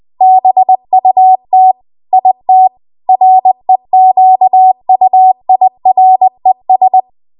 Morse code, heard as a keyed tone, about 20 wpm, 750 Hz. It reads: BUT IT REQUIRES
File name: Archive / 20 WPM / 20WPM